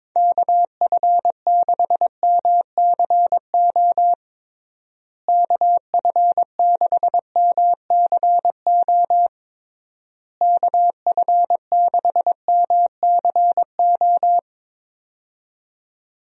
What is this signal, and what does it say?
Morse code, heard as a keyed tone, about 22 wpm, 700 Hz. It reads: KF6MCO KF6MCO KF6MCO
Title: USA Call Signs - 6 Characters - Repeated 3x - 22wpm